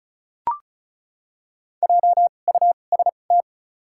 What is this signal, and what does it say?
Morse code, heard as a keyed tone, about 34 wpm, spaced as for 25 wpm, 700 Hz. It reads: JUST
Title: 2x Character Spacing - Top 200 Words - 35wpm